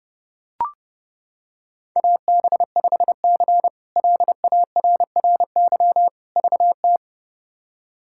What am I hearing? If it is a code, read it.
Morse code, heard as a keyed tone, about 30 wpm, 700 Hz. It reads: AB5C LARRY VT
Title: K1USN SST QSO Practice - 30wpm